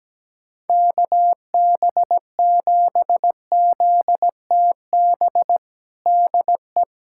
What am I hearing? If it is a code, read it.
Morse code, heard as a keyed tone, about 17 wpm, 700 Hz. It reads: KB7ZTB DE
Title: USA Callsign DE Callsign - 6 Characters - 17wpm